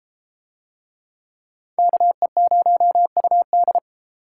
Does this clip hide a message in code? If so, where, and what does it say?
Yes: Morse code from 1.8 to 3.8 s, heard as a keyed tone, about 30 wpm, 700 Hz. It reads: KE0UD